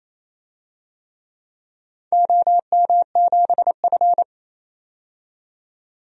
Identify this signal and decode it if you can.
Morse code, heard as a keyed tone, about 28 wpm, 700 Hz. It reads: OM7F
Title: DXCC-100 Call Signs - 4 Characters - 28wpm